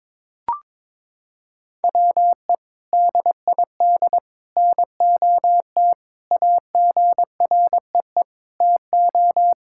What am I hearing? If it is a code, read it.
Morse code, heard as a keyed tone, about 22 wpm, 700 Hz. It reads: WE DID NOT AGREE TO
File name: practice / 22wpm / Sentences with 3-Word Phrases - 22wpm